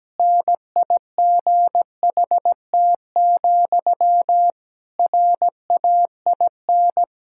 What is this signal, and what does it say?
Morse code, heard as a keyed tone, about 17 wpm, 700 Hz. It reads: NIGHT, RAIN
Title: Sentences from Top 1000 Words - Review Part 6 - 17wpm